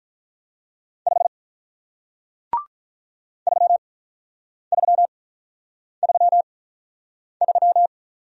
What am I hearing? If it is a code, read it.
Morse code, heard as a keyed tone, about 48 wpm, 700 Hz. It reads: 5 3 3 3 3